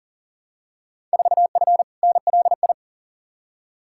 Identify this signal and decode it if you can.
Morse code, heard as a keyed tone, about 40 wpm, 700 Hz. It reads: 4F NLI
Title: Field Day Exchanges Encore - 40wpm